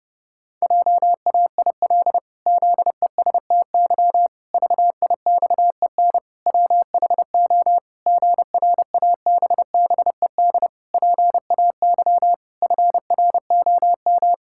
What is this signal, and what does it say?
Morse code, heard as a keyed tone, about 30 wpm, 700 Hz. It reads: JAIL ZESTY VIXEN WHO GRABBED PAY FROM
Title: Pangram Challenge v2 - 30wpm